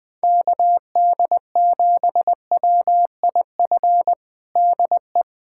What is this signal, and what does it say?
Morse code, heard as a keyed tone, about 21 wpm, 700 Hz. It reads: KD7WIF DE
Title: USA Callsign DE Callsign - 6 Characters - 20wpm